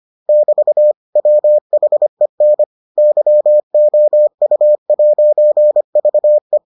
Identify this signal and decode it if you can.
Morse code, heard as a keyed tone, about 25 wpm, 600 Hz. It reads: <BT> WHEN YOU'VE